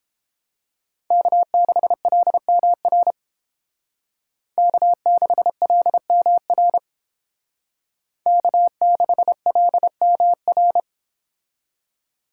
Morse code, heard as a keyed tone, about 30 wpm, 700 Hz. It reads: K6LMR K6LMR K6LMR